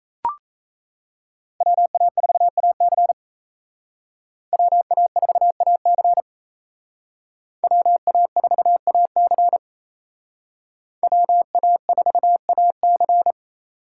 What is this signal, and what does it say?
Morse code, heard as a keyed tone, about 33 wpm, 700 Hz. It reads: WA4AC WA4AC WA4AC WA4AC
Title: Speed Racing - USA 2x2 Call Signs 28wpm